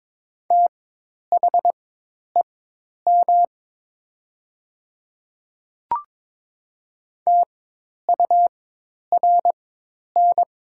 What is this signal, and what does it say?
Morse code, heard as a keyed tone, about 22 wpm, spaced as for 10 wpm, 700 Hz. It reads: THEM TURN
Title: 4x Character Spacing - Top 300 Words - 22wpm